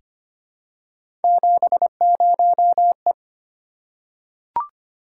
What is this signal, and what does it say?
Morse code, heard as a keyed tone, about 25 wpm, 700 Hz. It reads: 70E
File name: Three Letter-Number Combinations - 25wpm